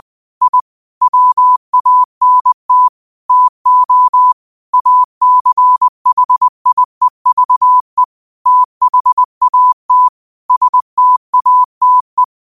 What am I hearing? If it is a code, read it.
Morse code, heard as a keyed tone, about 20 wpm, 1000 Hz. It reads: I WANT TO ACHIEVE THAT STATE